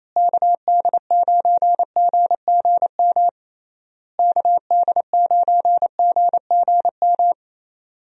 Morse code, heard as a keyed tone, about 28 wpm, 700 Hz. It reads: KD9GGM KD9GGM
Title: USA Call Signs - 6 Characters - Repeated 3x - 28wpm